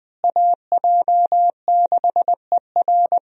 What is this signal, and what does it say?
Morse code, heard as a keyed tone, about 20 wpm, 700 Hz. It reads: AJ6ER
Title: USA Call Signs - 5 Characters - Repeated 3x - 20wpm